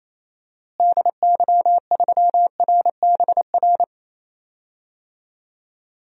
Morse code, heard as a keyed tone, about 28 wpm, 700 Hz. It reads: DY3RBR
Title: DXCC-100 Call Signs - 6 Characters - Repeated 2x - 28wpm